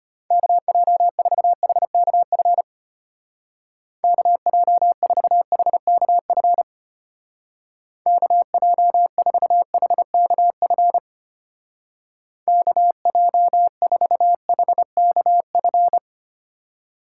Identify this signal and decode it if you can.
Morse code, heard as a keyed tone, about 32 wpm, 700 Hz. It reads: KJ4HKF KJ4HKF KJ4HKF KJ4HKF